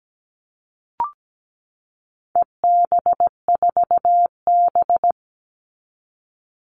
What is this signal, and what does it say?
Morse code, heard as a keyed tone, about 17 wpm, 700 Hz. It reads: EB4B
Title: All Cut Numbers - 4-Digit 17wpm